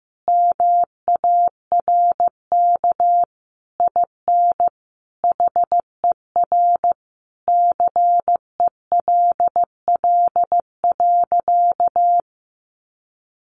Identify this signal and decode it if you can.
Morse code, heard as a keyed tone, about 15 wpm, 700 Hz. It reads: MARK IN HER CELL.
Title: Sentences from Top 1100 Words 15wpm